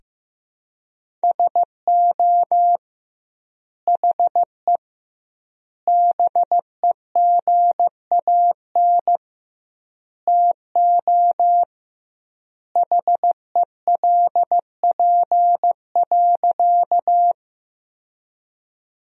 Morse code, heard as a keyed tone, about 15 wpm, 700 Hz. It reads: SO HE BEGAN TO HELP.
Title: 2x Word-Spacing - Sentences from Top 300 Words - 15wpm